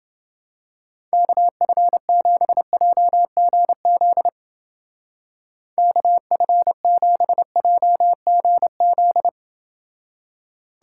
Morse code, heard as a keyed tone, about 28 wpm, 700 Hz. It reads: KF7JGZ KF7JGZ